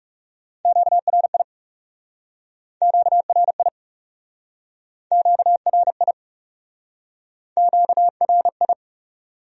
Morse code, heard as a keyed tone, about 40 wpm, 700 Hz. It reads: QRI QRI QRI QRI